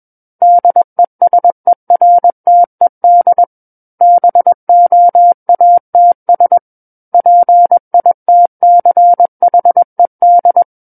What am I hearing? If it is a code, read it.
Morse code, heard as a keyed tone, about 21 wpm, 700 Hz. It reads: DESERTED BOATS PITCHED
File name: Book One / 12-Chapter 12 - 21 WPM